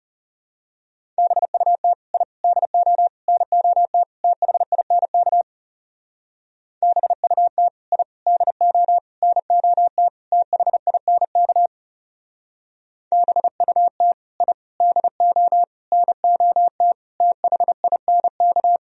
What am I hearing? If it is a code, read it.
Morse code, heard as a keyed tone, about 34 wpm, 700 Hz. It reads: BUT I DO NOT THINK BUT I DO NOT THINK BUT I DO NOT THINK
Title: Speed-Racing - Sets of 5 Words 30wpm